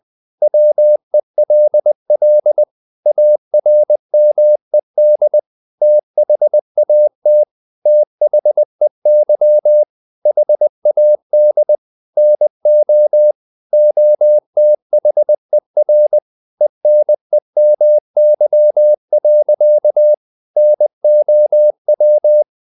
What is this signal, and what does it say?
Morse code, heard as a keyed tone, about 20 wpm, 600 Hz. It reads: WELL ARMED THAT THEY HAD NO OTHER ENEMY. NOW
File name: the old man and the sea part 2 20wpm